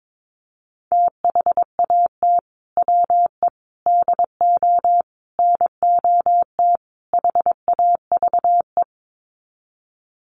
Morse code, heard as a keyed tone, about 22 wpm, 700 Hz. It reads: THAT WE DO NOT HAVE
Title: Sets of 5 Words Encore 22wpm